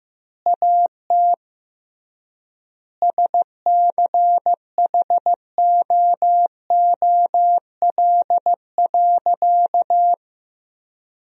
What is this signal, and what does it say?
Morse code, heard as a keyed tone, about 15 wpm, 700 Hz. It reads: AT SCHOOL.
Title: 3x Word-Spacing - Sentences from Top 300 Words - 15wpm